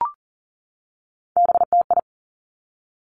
Morse code, heard as a keyed tone, about 40 wpm, 700 Hz. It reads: BTI
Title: Three Challenging Letter-Number Combinations 40wpm